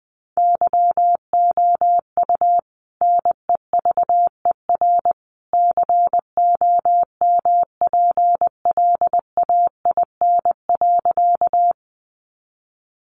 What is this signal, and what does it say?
Morse code, heard as a keyed tone, about 20 wpm, 700 Hz. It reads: YOU NEVER COMPLAIN.
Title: Sentences from Top 1300 Words 20wpm